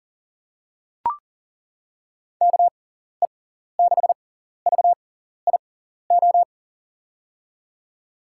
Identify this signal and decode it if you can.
Morse code, heard as a keyed tone, about 40 wpm, spaced as for 14 wpm, 700 Hz. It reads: KE6VIO
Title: USA Call Signs - 6 Characters - 6x Character Spacing - 40wpm